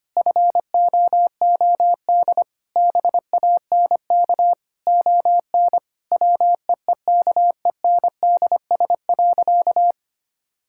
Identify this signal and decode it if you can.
Morse code, heard as a keyed tone, about 25 wpm, 700 Hz. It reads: FOOD BANK ON WEEKENDS.
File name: practice / 25wpm / Difficult Sentences with Hints - 25wpm